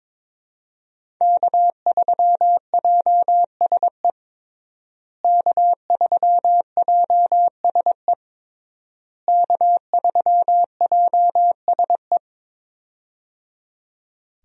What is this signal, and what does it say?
Morse code, heard as a keyed tone, about 22 wpm, 700 Hz. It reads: K3JSE K3JSE K3JSE
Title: USA Call Signs - 5 Characters - Repeated 3x - 22wpm